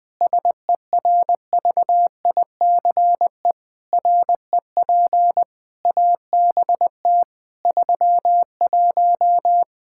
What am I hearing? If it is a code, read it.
Morse code, heard as a keyed tone, about 20 wpm, 700 Hz. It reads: SERVICE REP ABT 31
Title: Practice QSO - Exchange 3 - More Info - 20wpm